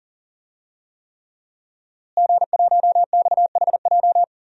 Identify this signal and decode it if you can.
Morse code, heard as a keyed tone, about 40 wpm, 700 Hz. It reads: G1XHJ
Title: DXCC-100 Call Signs - 5 Characters - Repeated 3x - 40wpm